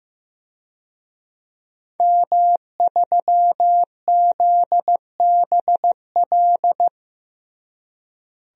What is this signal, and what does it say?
Morse code, heard as a keyed tone, about 15 wpm, 700 Hz. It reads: M3ZBL
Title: DXCC-100 Call Signs - 5 Characters - Repeated 2x - 15wpm